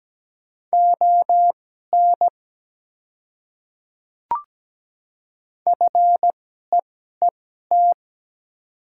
Morse code, heard as a keyed tone, about 17 wpm, spaced as for 12 wpm, 700 Hz. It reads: ON FEET
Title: 2x Character Spacing - Top 300 Words - 17wpm